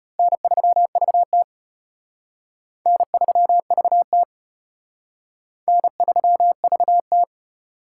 Morse code, heard as a keyed tone, about 32 wpm, 700 Hz. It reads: N3VT N3VT N3VT